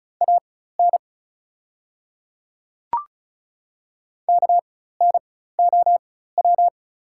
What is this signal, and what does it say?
Morse code, heard as a keyed tone, about 34 wpm, spaced as for 16 wpm, 700 Hz. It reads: AN KNOW
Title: 4x Character Spacing - Top 100 Words - 35wpm